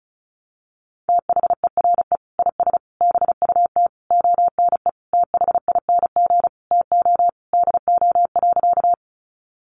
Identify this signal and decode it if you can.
Morse code, heard as a keyed tone, about 34 wpm, 700 Hz. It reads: THERE IS BUT ONE THING TO DO.
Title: Sentences from Top 100 Words 35wpm